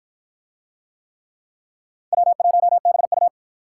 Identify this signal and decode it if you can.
Morse code, heard as a keyed tone, about 47 wpm, 700 Hz. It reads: W1BU